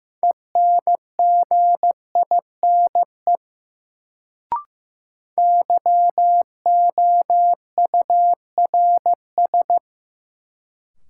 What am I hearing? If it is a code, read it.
Morse code, heard as a keyed tone, about 15 wpm, 700 Hz. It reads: ENGINE YOURS
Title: Sentences from Top 1500 Words - New Word Drill Alt - 15wpm